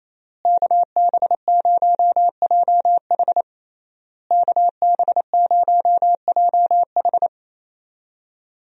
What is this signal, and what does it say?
Morse code, heard as a keyed tone, about 28 wpm, 700 Hz. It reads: KB0JH KB0JH
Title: USA Call Signs - 5 Characters - Repeated 3x - 28wpm